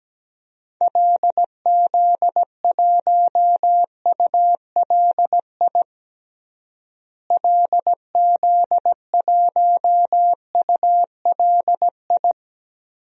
Morse code, heard as a keyed tone, about 17 wpm, 700 Hz. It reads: LZ1ULI LZ1ULI
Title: DXCC-100 Call Signs - 6 Characters - Repeated 2x - 17wpm